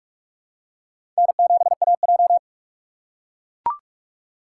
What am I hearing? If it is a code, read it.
Morse code, heard as a keyed tone, about 42 wpm, 700 Hz. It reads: N7AJ
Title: USA Call Signs - 4 Characters - 6x Character Spacing - 45wpm